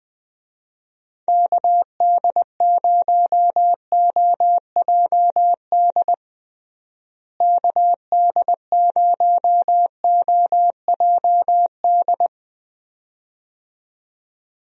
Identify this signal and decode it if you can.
Morse code, heard as a keyed tone, about 21 wpm, 700 Hz. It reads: KD0OJD KD0OJD
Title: USA Call Signs - 6 Characters - Repeated 2x - 20wpm